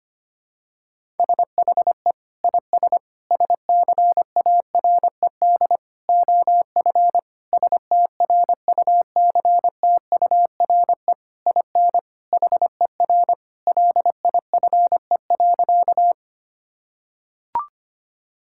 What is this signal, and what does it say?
Morse code, heard as a keyed tone, about 25 wpm, 700 Hz. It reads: SHE IS SCARED OF STRUCTURE IN HER LIFE.
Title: Sentences from Top 1000 Words - Review Part 6 - 25wpm